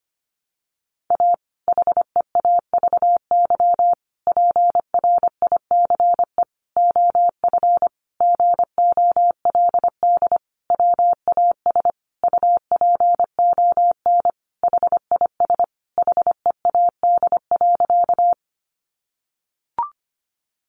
Morse code, heard as a keyed tone, about 25 wpm, 700 Hz. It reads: A HEAVY PRICE OF GOLD WAS UPON HIS HEAD.
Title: Sentences from Top 1100 Words 25wpm